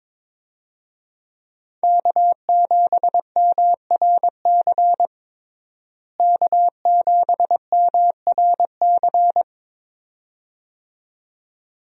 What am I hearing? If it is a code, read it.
Morse code, heard as a keyed tone, about 21 wpm, 700 Hz. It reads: K7MRC K7MRC
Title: USA Call Signs - 5 Characters - Repeated 2x - 22wpm